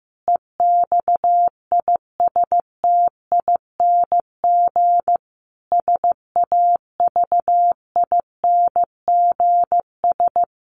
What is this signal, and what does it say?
Morse code, heard as a keyed tone, about 15 wpm, 700 Hz. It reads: EXISTING SAVINGS
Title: Sentences from Top 1500 Words 15wpm